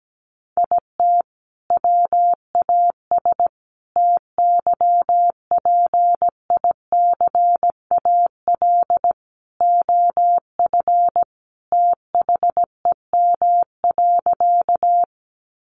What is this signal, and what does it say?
Morse code, heard as a keyed tone, about 17 wpm, 700 Hz. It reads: IT WAS TYPICAL OF THEM.
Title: Sentences from Top 1100 Words 17wpm